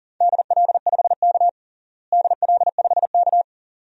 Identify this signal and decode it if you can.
Morse code, heard as a keyed tone, about 37 wpm, 700 Hz. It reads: DL5K DL5K
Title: DXCC-100 Call Signs - 4 Characters - Repeated 2x - 40wpm